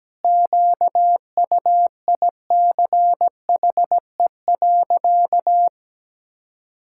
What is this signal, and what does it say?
Morse code, heard as a keyed tone, about 17 wpm, 700 Hz. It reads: QUICHE.
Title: Pangram Challenge v2 - 17wpm